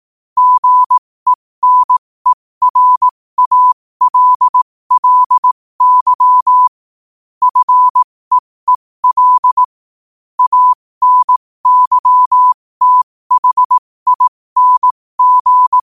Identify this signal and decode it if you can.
Morse code, heard as a keyed tone, about 18 wpm, spaced as for 15 wpm, 1000 Hz. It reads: GENERALLY FEEL ANYTHING
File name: qotdfiles / 15wpm